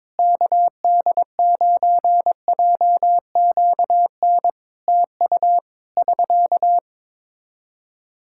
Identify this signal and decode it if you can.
Morse code, heard as a keyed tone, about 22 wpm, 700 Hz. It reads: KD9JQN TU <SK>
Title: Practice QSO - Exchange 4 - Ending - 22wpm